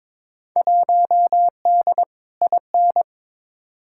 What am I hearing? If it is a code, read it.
Morse code, heard as a keyed tone, about 22 wpm, 700 Hz. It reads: 1D IN
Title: Field Day Exchanges - 22wpm